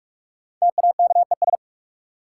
Morse code, heard as a keyed tone, about 46 wpm, 700 Hz. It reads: TAKES
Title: Sentences from Top 1000 Words - Review Part 6 - New Word Drill - 45wpm